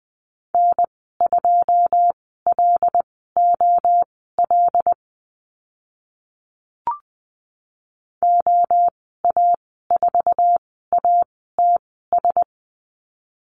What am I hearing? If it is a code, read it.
Morse code, heard as a keyed tone, about 20 wpm, spaced as for 14 wpm, 700 Hz. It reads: N2LOL OA4ATS
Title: 082 Lesson 14g - Number 2 - Call Signs 20wpm